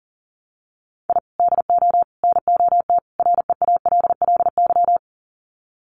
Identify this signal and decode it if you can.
Morse code, heard as a keyed tone, about 40 wpm, 700 Hz. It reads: I DO NOT REALLY
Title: Sets of 4 Words 40wpm